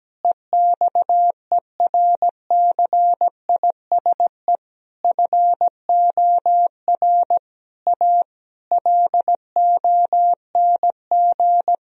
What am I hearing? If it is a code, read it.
Morse code, heard as a keyed tone, about 17 wpm, 700 Hz. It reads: EXERCISE FOR A LONG
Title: Sentences from Top 1000 Words - Review Part 4 - 17wpm